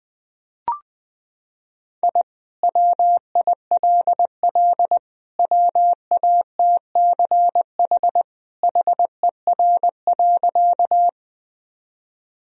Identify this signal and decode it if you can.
Morse code, heard as a keyed tone, about 20 wpm, 700 Hz. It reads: I WILL WATCH HER.
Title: Sentences from Top 1000 Words - Review Part 1 - 20wpm